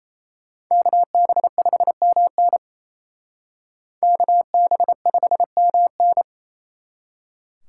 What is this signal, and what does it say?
Morse code, heard as a keyed tone, about 35 wpm, 700 Hz. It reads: KB5MN KB5MN